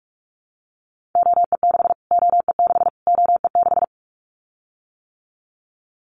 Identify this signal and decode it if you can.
Morse code, heard as a keyed tone, about 44 wpm, 700 Hz. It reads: OE6 OE6 OE6
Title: Callsign Prefixes for Top 25 DXCC - Repeated 3x - 45wpm